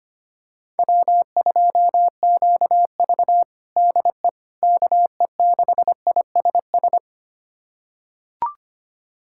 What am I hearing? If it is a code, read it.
Morse code, heard as a keyed tone, about 25 wpm, 700 Hz. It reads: W2QV DE KE6ISS
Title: USA Callsign DE Callsign 25wpm